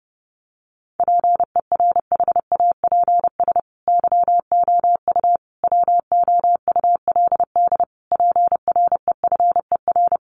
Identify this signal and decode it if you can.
Morse code, heard as a keyed tone, about 30 wpm, 700 Hz. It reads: PERHAPS YOU WOULD PREFER
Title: Sentences from Top 1100 Words 30wpm